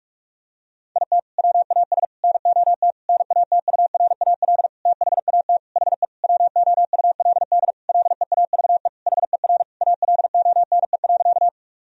Morse code, heard as a keyed tone, about 46 wpm, 700 Hz. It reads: IT WAS NOT NATURAL THAT HE WOULD LEAVE HER ALONE.
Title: Sentences from Top 1000 Words - Review Part 4 - 45wpm